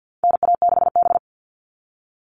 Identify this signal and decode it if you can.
Morse code, heard as a keyed tone, about 50 wpm, 700 Hz. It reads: NA6B
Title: All Cut Numbers - 4-Digit 50wpm